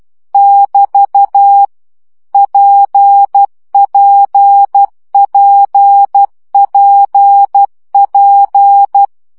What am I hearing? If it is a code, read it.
Morse code, heard as a keyed tone, about 12 wpm, 800 Hz. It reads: <BT> PPPPP
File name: CW gyakorlas p 10 baud